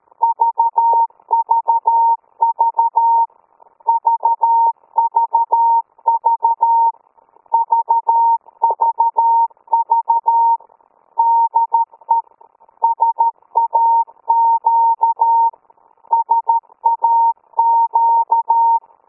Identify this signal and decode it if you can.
Morse code, heard as a keyed tone, about 13 wpm, 920 Hz. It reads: VVV VVV VVV DE SAQ SAQ